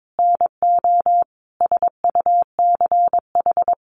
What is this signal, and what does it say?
Morse code, heard as a keyed tone, about 22 wpm, 700 Hz. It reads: NO SUCH
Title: Sentences from Top 600 Words 22wpm